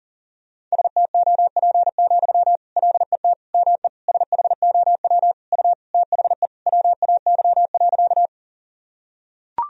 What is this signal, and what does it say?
Morse code, heard as a keyed tone, about 40 wpm, 700 Hz. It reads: STOP, LET ME SHOW U THE WAY.
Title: Sentences from Top 300 Words and Abbreviations 40wpm